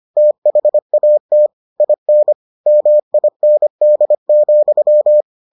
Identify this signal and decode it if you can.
Morse code, heard as a keyed tone, about 25 wpm, 600 Hz. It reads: THAT IN MIND,